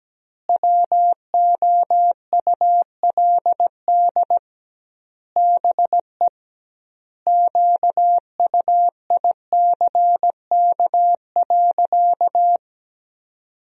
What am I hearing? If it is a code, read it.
Morse code, heard as a keyed tone, about 17 wpm, 700 Hz. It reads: WOULD BE QUICK.
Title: 2x Word-Spacing - Sentences from Top 500 Words - 17wpm